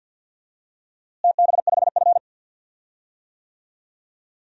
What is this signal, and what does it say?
Morse code, heard as a keyed tone, about 52 wpm, 700 Hz. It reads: TB5F
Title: DXCC-100 Call Signs - 4 Characters - Repeated 3x - 50wpm